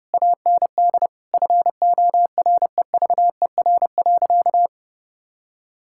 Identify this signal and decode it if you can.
Morse code, heard as a keyed tone, about 30 wpm, 700 Hz. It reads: AND FOREVER.
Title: Sentences from Top 1600 Words 30wpm